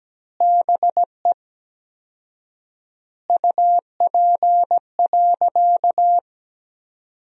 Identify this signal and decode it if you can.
Morse code, heard as a keyed tone, about 17 wpm, 700 Hz. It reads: BE UP.
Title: 4x Word-Spacing - Sentences from Top 300 Words - 17wpm